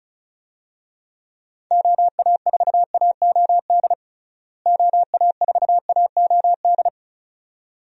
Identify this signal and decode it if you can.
Morse code, heard as a keyed tone, about 35 wpm, 700 Hz. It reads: OA4AOD OA4AOD
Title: USA Call Signs - 6 Characters - Repeated 2x - 35wpm